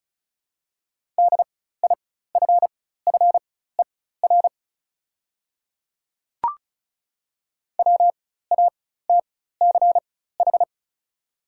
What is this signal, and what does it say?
Morse code, heard as a keyed tone, about 36 wpm, spaced as for 16 wpm, 700 Hz. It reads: DIFFER WATCH
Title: 4x Character Spacing - Top 300 Words - 35wpm